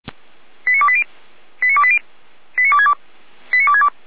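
Five-tone selective call (ZVEI), heard as quick runs of five tones, tones of about 70 ms. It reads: 89280 89280 89262 89262
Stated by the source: ZVEI five-tone selective call received by SDR.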